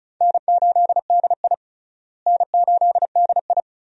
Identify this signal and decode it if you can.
Morse code, heard as a keyed tone, about 34 wpm, 700 Hz. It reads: N8DI N8DI
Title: USA Call Signs - 4 Characters - Repeated 3x - 35wpm